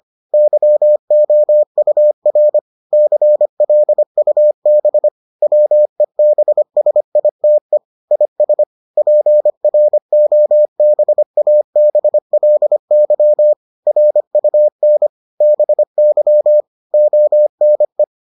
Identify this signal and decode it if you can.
Morse code, heard as a keyed tone, about 25 wpm, 600 Hz. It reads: YOUR CLUB WEBSITE IS PROBABLY RUN BY ONE